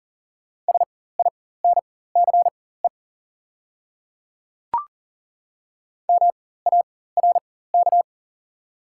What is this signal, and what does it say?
Morse code, heard as a keyed tone, about 37 wpm, spaced as for 18 wpm, 700 Hz. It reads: SINCE MARK
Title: 4x Character Spacing - Top 300 Words - 40wpm